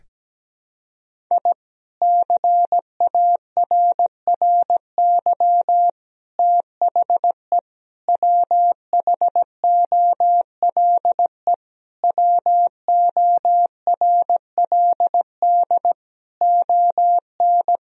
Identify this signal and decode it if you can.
Morse code, heard as a keyed tone, about 17 wpm, 700 Hz. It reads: I CARRY THE WHOLE WORLD ON
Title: Sentences from Top 1000 Words - Review Part 1 - 17wpm